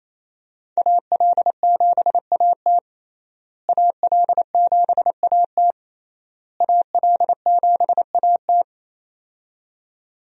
Morse code, heard as a keyed tone, about 27 wpm, 700 Hz. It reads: AL7AT AL7AT AL7AT
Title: USA Call Signs - 5 Characters - Repeated 3x - 28wpm